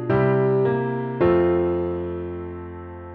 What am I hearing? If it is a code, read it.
Dial tone (North America), 350 + 440 Hz.